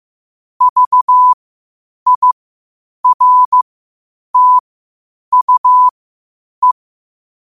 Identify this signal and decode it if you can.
Morse code, heard as a keyed tone, about 15 wpm, spaced as for 8 wpm, 1000 Hz. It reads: VIRTUE
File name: qotdfiles / 8wpm